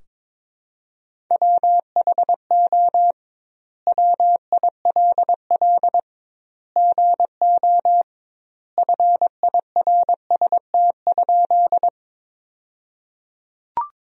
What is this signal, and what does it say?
Morse code, heard as a keyed tone, about 22 wpm, 700 Hz. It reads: WHO WILL GO FIRST?
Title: 2x Word-Spacing - Sentences from Top 100 Words - 22wpm